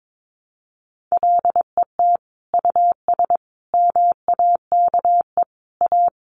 Morse code, heard as a keyed tone, about 22 wpm, 700 Hz. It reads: LET US MAKE A
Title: Sentences from Top 1200 Words 22wpm